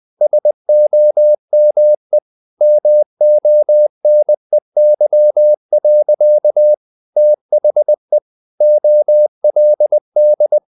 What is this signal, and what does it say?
Morse code, heard as a keyed tone, about 20 wpm, 600 Hz. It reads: SOME MONEY. THE OLD
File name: the old man and the sea / the old man and the sea part 1 20wpm